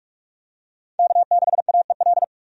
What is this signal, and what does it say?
Morse code, heard as a keyed tone, about 46 wpm, 700 Hz. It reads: K6AEL
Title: USA Call Signs - 5 Characters - 45wpm